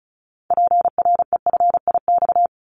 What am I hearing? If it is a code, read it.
Morse code, heard as a keyed tone, about 34 wpm, 700 Hz. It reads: PREFIX
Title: Callsign Prefixes for Top 100 DXCC - 35wpm